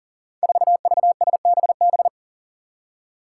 Morse code, heard as a keyed tone, about 42 wpm, 700 Hz. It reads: 4VSBB
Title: Five Challenging Letter-Number Combinations 40wpm